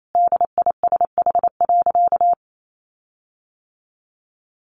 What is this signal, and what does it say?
Morse code, heard as a keyed tone, about 28 wpm, 700 Hz. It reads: DISH.
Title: Sentences from Top 1300 Words 28wpm